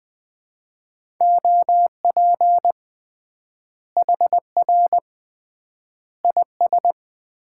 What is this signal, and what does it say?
Morse code, heard as a keyed tone, about 20 wpm, 700 Hz. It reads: OP HR IS
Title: Common QSO Phrases - 3x Word-Spacing - 20wpm